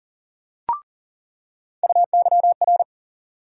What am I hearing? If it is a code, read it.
Morse code, heard as a keyed tone, about 38 wpm, 700 Hz. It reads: UYR
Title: USA Call Sign Postfixes - 3 Letters 40wpm